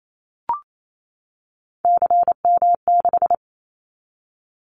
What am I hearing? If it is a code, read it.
Morse code, heard as a keyed tone, about 28 wpm, 700 Hz. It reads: CM6
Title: Callsign Prefixes for Top 100 DXCC - 28wpm